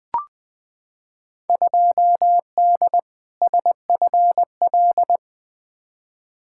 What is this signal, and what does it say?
Morse code, heard as a keyed tone, about 20 wpm, 700 Hz. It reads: 2D SFL
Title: Field Day Exchanges Encore - 20wpm